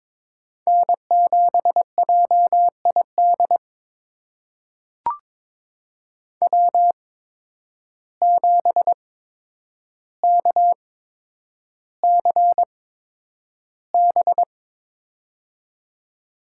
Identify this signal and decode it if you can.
Morse code, heard as a keyed tone, about 22 wpm, spaced as for 9 wpm, 700 Hz. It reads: N7JID W7KCB
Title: USA Call Signs - 5 Characters - 8x Character Spacing - 22wpm